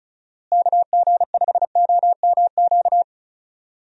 Morse code, heard as a keyed tone, about 36 wpm, 700 Hz. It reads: KG5OMQ
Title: USA Call Signs - 6 Characters - Repeated 2x - 35wpm